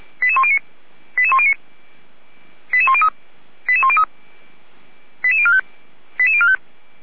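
Five-tone selective call (ZVEI), heard as quick runs of five tones, tones of about 70 ms. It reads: 88109 88109 88103 88103 88045 88045
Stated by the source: ZVEI five-tone selective call received with an SDR.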